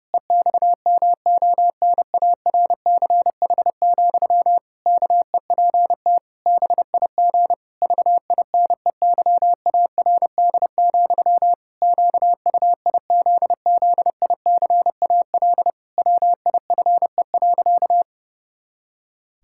Morse code, heard as a keyed tone, about 30 wpm, 700 Hz. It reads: EXMONARCH, KEPT BIG VINEYARD, QUIZZICAL WIFE.
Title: Pangram Challenge v2 - 30wpm